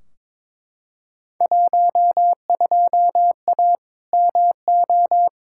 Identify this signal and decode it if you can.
Morse code, heard as a keyed tone, about 21 wpm, 700 Hz. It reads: 12A MO
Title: Field Day Exchanges Encore - 22wpm